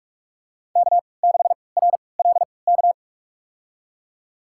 Morse code, heard as a keyed tone, about 47 wpm, spaced as for 26 wpm, 700 Hz. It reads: K6RLK
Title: USA Call Signs - 5 Characters - 3x Character Spacing - 45wpm